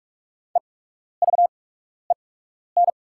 Morse code, heard as a keyed tone, about 41 wpm, spaced as for 12 wpm, 700 Hz. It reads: EVEN
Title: 8x Character Spacing - Top 200 Words - 45wpm